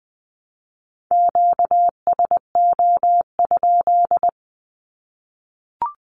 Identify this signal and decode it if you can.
Morse code, heard as a keyed tone, about 20 wpm, 700 Hz. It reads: QSO?
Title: Q-Codes 20wpm